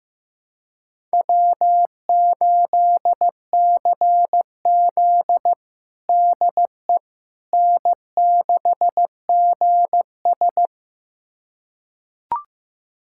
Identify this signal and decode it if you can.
Morse code, heard as a keyed tone, about 15 wpm, 700 Hz. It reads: W8CZ DE N6GS
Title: USA Callsign DE Callsign - 4 Characters - 15wpm